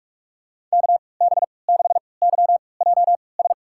Morse code, heard as a keyed tone, about 44 wpm, spaced as for 25 wpm, 700 Hz. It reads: KB6YJS
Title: USA Call Signs - 3x Character Spacing - 45wpm